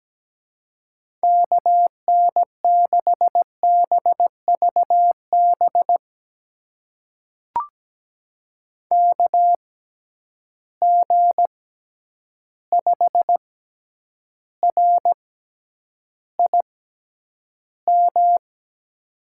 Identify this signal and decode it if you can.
Morse code, heard as a keyed tone, about 17 wpm, spaced as for 9 wpm, 700 Hz. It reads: KN6BVB KG5RIM
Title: USA Call Signs - 6 Characters - 6x Character Spacing - 17wpm